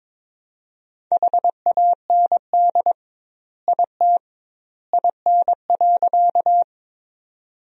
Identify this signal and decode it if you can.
Morse code, heard as a keyed tone, about 21 wpm, 700 Hz. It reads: HAND IT IN.
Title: 2x Word-Spacing - Sentences from Top 200 Words - 22wpm